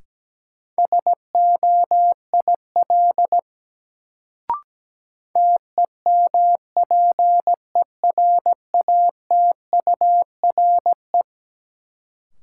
Morse code, heard as a keyed tone, about 17 wpm, 700 Hz. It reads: SOIL TEMPERATURE
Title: Sentences from Top 1000 Words - Review Part 4 - New Word Drill - 17wpm